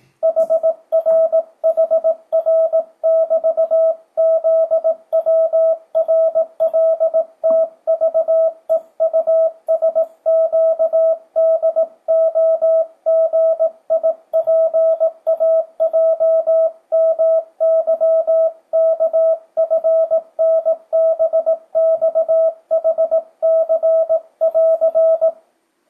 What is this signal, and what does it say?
Morse code, heard as a keyed tone, about 18 wpm, 650 Hz, with radio noise. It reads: HRHR<BT>ZWRLTVEUSQDOGIPAJMYKFNBXHC<AR>
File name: 2025cw / speedup / 18wpm